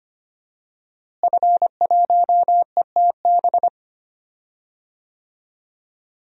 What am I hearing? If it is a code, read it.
Morse code, heard as a keyed tone, about 25 wpm, 700 Hz. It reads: F1ETB
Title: DXCC-100 Call Signs - 5 Characters - Repeated 3x - 25wpm